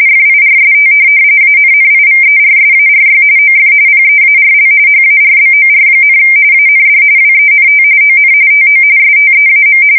RTTY, about 45.45 baud. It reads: FIRST ANNIVERSARY OF PEARL HARBOR, AND WENT ON TO STEAM
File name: fsk170 / USS-New-Jersey